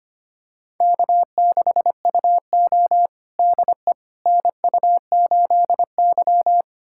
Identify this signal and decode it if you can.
Morse code, heard as a keyed tone, about 25 wpm, 700 Hz. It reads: K6UO DE NU8Y
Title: USA Callsign DE Callsign - 4 Characters - 25wpm